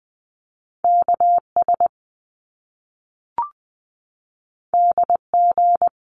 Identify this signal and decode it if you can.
Morse code, heard as a keyed tone, about 20 wpm, 700 Hz. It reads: KS DG
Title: Top Two Letter Combinations in Words 20wpm